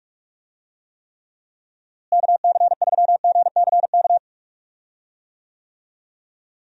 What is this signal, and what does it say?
Morse code, heard as a keyed tone, about 46 wpm, 700 Hz. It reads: KC3GCK
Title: USA Call Signs - 6 Characters - Repeated 3x - 45wpm